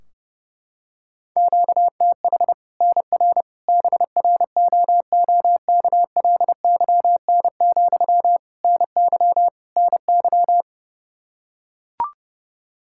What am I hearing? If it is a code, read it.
Morse code, heard as a keyed tone, about 32 wpm, 700 Hz. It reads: QTH NR BROOKLYN, NY NY
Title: Practice QSO - Exchange 1 - Beginning - 30wpm